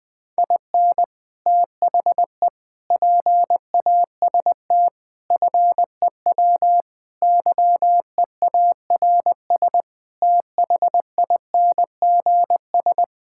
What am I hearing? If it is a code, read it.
Morse code, heard as a keyed tone, about 21 wpm, 700 Hz. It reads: IN THE PAST FEW YEARS THINGS
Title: Sentences with 5-Word Phrases - 20wpm